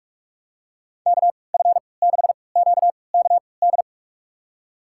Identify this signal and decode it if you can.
Morse code, heard as a keyed tone, about 46 wpm, spaced as for 25 wpm, 700 Hz. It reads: KF6QKD
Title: USA Call Signs - 6 Characters - 3x Character Spacing - 45wpm